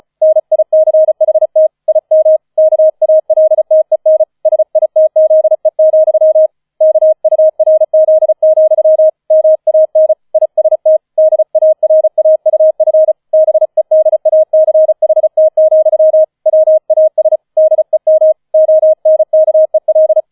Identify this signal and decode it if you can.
Morse code, heard as a keyed tone, about 35 wpm, 630 Hz. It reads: NICHT IM KALTEN SITZE, KURZ, MAN IST DARAUF BEDACHT, WAS DEM ONKEL
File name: 005 Fünfter Streich 35wpm